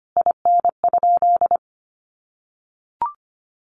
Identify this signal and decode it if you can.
Morse code, heard as a keyed tone, about 25 wpm, 700 Hz. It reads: IN?
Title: Sentences from Top 300 Words 25wpm